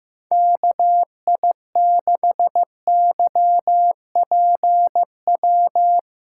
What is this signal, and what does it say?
Morse code, heard as a keyed tone, about 15 wpm, 700 Hz. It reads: KI6YPW
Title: USA Callsign DE Callsign 15wpm